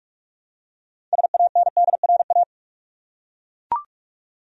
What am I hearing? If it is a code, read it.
Morse code, heard as a keyed tone, about 41 wpm, 700 Hz. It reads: SANDRA
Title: USA Common Names - 45wpm